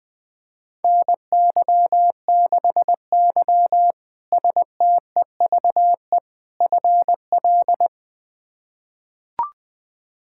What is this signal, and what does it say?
Morse code, heard as a keyed tone, about 21 wpm, 700 Hz. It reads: NY6Y STEVE FL
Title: K1USN SST QSO Practice - 20wpm